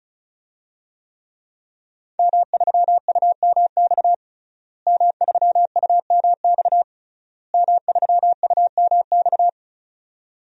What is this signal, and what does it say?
Morse code, heard as a keyed tone, about 36 wpm, 700 Hz. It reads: M3UMX M3UMX M3UMX
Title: DXCC-100 Call Signs - 5 Characters - Repeated 3x - 35wpm